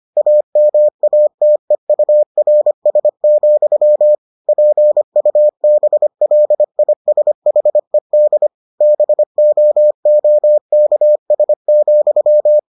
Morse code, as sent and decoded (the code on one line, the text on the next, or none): .- -- .- - . ..- .-. ... --..-- / .--. ..- -... .-.. .. ... .... -.. / -... --- --- -.- ... --..--
AMATEURS, PUBLISHED BOOKS,